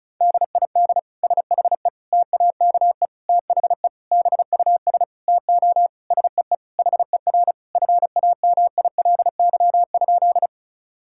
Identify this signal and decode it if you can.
Morse code, heard as a keyed tone, about 36 wpm, 700 Hz. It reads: DID SHE TAKE THE BUS TO SEE HER FAMILY?
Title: Sentences from Top 1000 Words - Review Part 6 - 35wpm